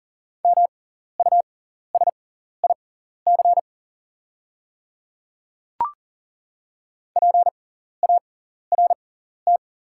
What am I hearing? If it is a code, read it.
Morse code, heard as a keyed tone, about 40 wpm, spaced as for 14 wpm, 690 Hz. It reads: MUSIC PART
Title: 6x Character Spacing - Top 500 Words - 40wpm